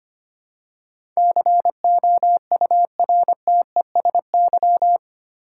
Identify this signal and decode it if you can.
Morse code, heard as a keyed tone, about 25 wpm, 700 Hz. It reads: COURTESY
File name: 8 Letter Words - First 3 Letters Given - 25wpm